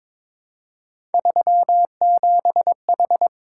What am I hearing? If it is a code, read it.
Morse code, heard as a keyed tone, about 22 wpm, 700 Hz. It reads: 37H
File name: practice / 22wpm / Three Letter-Number Combinations - 22wpm